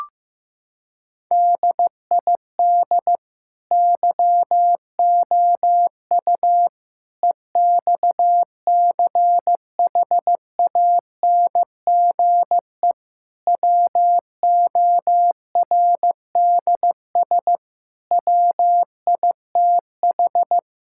Morse code, heard as a keyed tone, about 15 wpm, 700 Hz. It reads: DID YOU EXCHANGE WORDS WITH
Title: Sentences from Top 1000 Words - Review Part 6 - 15wpm